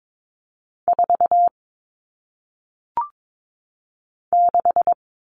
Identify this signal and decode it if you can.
Morse code, heard as a keyed tone, about 22 wpm, 700 Hz. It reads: 4 6
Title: All Cut Numbers - 1-Digit 22wpm